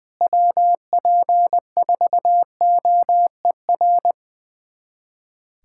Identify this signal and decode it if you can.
Morse code, heard as a keyed tone, about 21 wpm, 700 Hz. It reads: WP4OER
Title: USA Callsign DE Callsign - 6 Characters - 20wpm